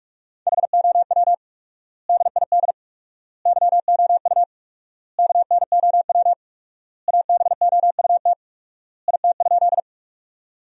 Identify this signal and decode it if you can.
Morse code, heard as a keyed tone, about 47 wpm, 700 Hz. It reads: HOW DID YOU KNOW ABOUT IT?
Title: 4x Word-Spacing - Sentences from Top 100 Words - 45wpm